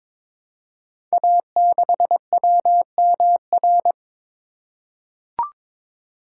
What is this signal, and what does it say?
Morse code, heard as a keyed tone, about 22 wpm, 700 Hz. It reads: A6WMR
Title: Code Groups - 5 Letter:Numbers - 22wpm